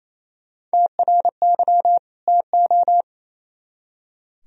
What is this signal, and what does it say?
Morse code, heard as a keyed tone, about 34 wpm, 700 Hz. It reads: TRY TO
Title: Speed-Racing - Sets of 2 Words 28wpm